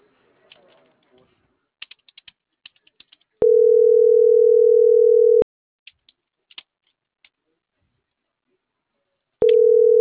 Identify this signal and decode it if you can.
Ringback tone (North America), 440 + 480 Hz.